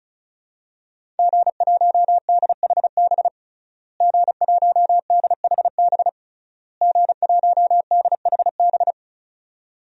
Morse code, heard as a keyed tone, about 36 wpm, 700 Hz. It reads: G1DHB G1DHB G1DHB
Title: DXCC-100 Call Signs - 5 Characters - Repeated 3x - 35wpm